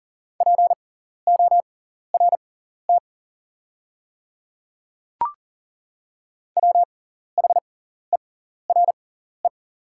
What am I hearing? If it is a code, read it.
Morse code, heard as a keyed tone, about 40 wpm, spaced as for 14 wpm, 690 Hz. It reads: PORT WHERE
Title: 6x Character Spacing - Top 200 Words - 40wpm